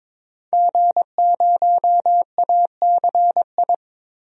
Morse code, heard as a keyed tone, about 21 wpm, 700 Hz. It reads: G0ACI
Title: DXCC-100 Call Signs - 5 Characters - 22wpm